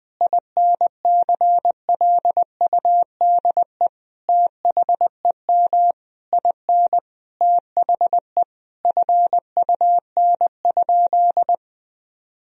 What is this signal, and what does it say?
Morse code, heard as a keyed tone, about 20 wpm, 700 Hz. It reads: INCLUDE THEM IN THE FUN?
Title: Sentences from Top 1000 Words - Review Part 3 - 20wpm